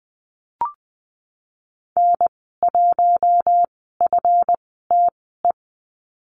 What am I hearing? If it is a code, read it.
Morse code, heard as a keyed tone, about 19 wpm, spaced as for 14 wpm, 700 Hz. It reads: N1FTE
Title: 140 Lesson 23g - Letter f - Call Signs 20wpm